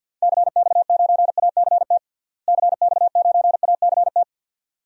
Morse code, heard as a keyed tone, about 52 wpm, 700 Hz. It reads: CX9ACT CX9ACT
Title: DXCC-100 Call Signs - 6 Characters - Repeated 2x - 50wpm